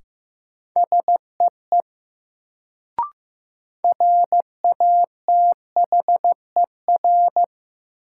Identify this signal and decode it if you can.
Morse code, heard as a keyed tone, about 15 wpm, 700 Hz. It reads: SEE RATHER
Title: Sets of 2 Words - New Word Drill - 15wpm